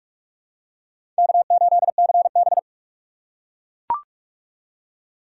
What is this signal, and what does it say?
Morse code, heard as a keyed tone, about 41 wpm, 700 Hz. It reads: K8CB